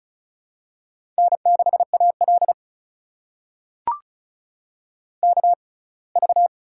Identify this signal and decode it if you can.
Morse code, heard as a keyed tone, about 36 wpm, spaced as for 24 wpm, 700 Hz. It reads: N6AL KV
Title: USA Call Signs - 4 Characters - 6x Character Spacing - 35wpm